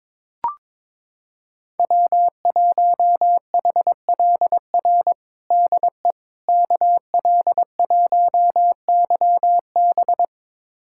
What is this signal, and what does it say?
Morse code, heard as a keyed tone, about 21 wpm, 700 Hz. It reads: W1HLR DE KL1YB
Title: USA Callsign DE Callsign - 5 Characters - 22wpm